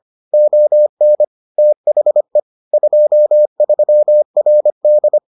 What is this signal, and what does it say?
Morse code, heard as a keyed tone, about 25 wpm, 600 Hz. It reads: ON THE 23RD